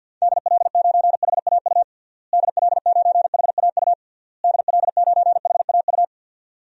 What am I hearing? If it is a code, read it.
Morse code, heard as a keyed tone, about 53 wpm, 700 Hz. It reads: DL9HAU DL9HAU DL9HAU
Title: DXCC-100 Call Signs - 6 Characters - Repeated 3x - 50wpm